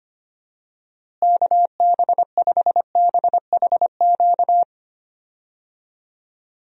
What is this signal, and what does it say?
Morse code, heard as a keyed tone, about 25 wpm, 700 Hz. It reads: KB5BHQ
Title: USA Call Signs - 6 Characters - Repeated 3x - 25wpm